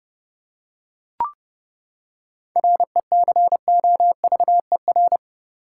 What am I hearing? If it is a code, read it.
Morse code, heard as a keyed tone, about 31 wpm, 700 Hz. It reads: RECOVER
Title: Seven Letter Words - 30wpm